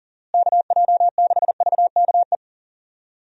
Morse code, heard as a keyed tone, about 40 wpm, 700 Hz. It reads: KJ6VKE
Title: USA Call Signs - 6 Characters - Repeated 3x - 40wpm